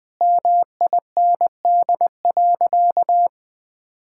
Morse code, heard as a keyed tone, about 21 wpm, 700 Hz. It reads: MIND.
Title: Sentences with 4-Word Phrases - 20wpm